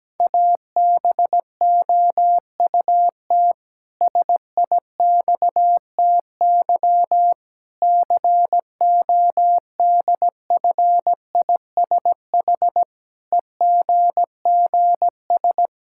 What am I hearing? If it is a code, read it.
Morse code, heard as a keyed tone, about 17 wpm, 700 Hz. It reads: ABOUT SIXTY CODFISH EGGS
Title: Pangram Challenge v2 - 17wpm